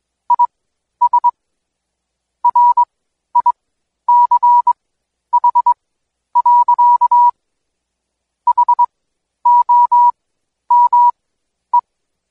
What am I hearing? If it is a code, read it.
Morse code, heard as a keyed tone, about 22 wpm, spaced as for 11 wpm, 970 Hz. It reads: IS RICH. HOME
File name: Code Quick / AudioDownld2 / Seg120-Day 25-DayDrill 25